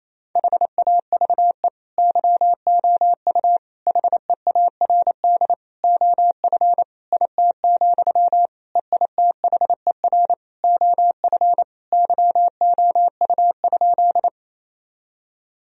Morse code, heard as a keyed tone, about 28 wpm, 700 Hz. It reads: HAVE YOU HEARD OF IT, EITHER OF YOU?
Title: Sentences from Top 1000 Words - Review Part 4 - 28wpm